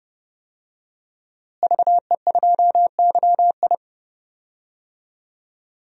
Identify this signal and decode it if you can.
Morse code, heard as a keyed tone, about 30 wpm, 700 Hz. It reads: VE2YI
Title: DXCC-100 Call Signs - 5 Characters - Repeated 3x - 30wpm